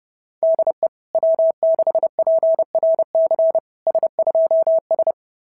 Morse code, heard as a keyed tone, about 30 wpm, 690 Hz. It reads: DE W6PRC S2S
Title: SOTA QSO Practice 30wpm